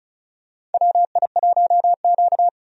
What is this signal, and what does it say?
Morse code, heard as a keyed tone, about 36 wpm, 700 Hz. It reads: WI1Q
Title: USA Call Signs - 4 Characters - 35wpm